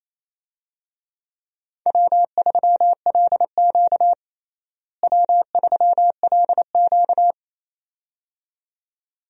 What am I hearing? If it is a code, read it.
Morse code, heard as a keyed tone, about 28 wpm, 700 Hz. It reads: W3LQ W3LQ
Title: USA Call Signs - 4 Characters - Repeated 2x - 28wpm